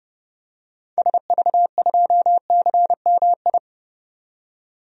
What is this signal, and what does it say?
Morse code, heard as a keyed tone, about 30 wpm, 700 Hz. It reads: SV2CMI
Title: DXCC-100 Call Signs - 6 Characters - 30wpm